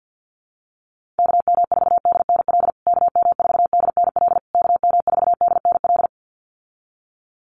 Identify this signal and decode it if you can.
Morse code, heard as a keyed tone, about 50 wpm, 700 Hz. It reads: KM4DNL KM4DNL KM4DNL
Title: USA Call Signs Repeated 3x 50wpm